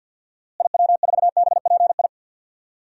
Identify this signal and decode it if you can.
Morse code, heard as a keyed tone, about 50 wpm, 700 Hz. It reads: IW4BPI
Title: DXCC-100 Call Signs - 6 Characters - Repeated 2x - 50wpm